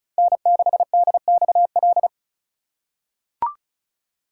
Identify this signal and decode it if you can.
Morse code, heard as a keyed tone, about 36 wpm, 700 Hz. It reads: N6DXL
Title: USA Call Signs - 5 Characters - 2x Character Spacing - 35wpm